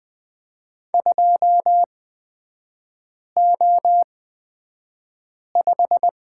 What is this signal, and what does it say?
Morse code, heard as a keyed tone, about 20 wpm, spaced as for 5 wpm, 700 Hz. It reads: 2O5
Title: 255 Lesson 41e - Period - Three Characters 20wpm